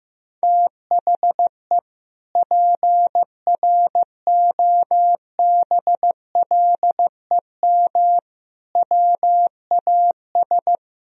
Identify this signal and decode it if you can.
Morse code, heard as a keyed tone, about 15 wpm, 700 Hz. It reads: THE PROBLEM WAS
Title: Sentences from Top 1000 Words - Review Part 2 - 15wpm